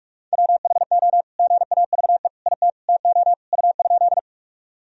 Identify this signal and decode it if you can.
Morse code, heard as a keyed tone, about 46 wpm, 700 Hz. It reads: WHO GAVE IT TO U?
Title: Sentences from Top 500 Words and Abbreviations 45wpm